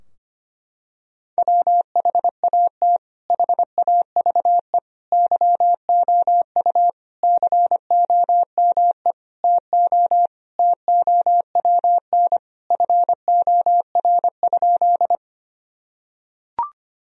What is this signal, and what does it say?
Morse code, heard as a keyed tone, about 25 wpm, 700 Hz. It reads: WHAT HAVE YOU COME TO TOWN FOR?
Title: Sentences from Top 1000 Words - Review Part 2 - 25wpm